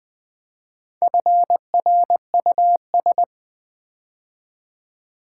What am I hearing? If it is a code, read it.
Morse code, heard as a keyed tone, about 20 wpm, 700 Hz. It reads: FRUS
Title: Build A Word - 4 Letters At A Time - 11-Letter Words - 20wpm